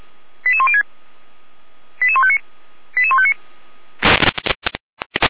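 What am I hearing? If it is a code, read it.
Five-tone selective call (ZVEI), heard as quick runs of five tones, tones of about 70 ms. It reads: 88106 88159 88159
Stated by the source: ZVEI five-tone selective call received on SDR, from feuerwehr-orlamuende.org fire brigade.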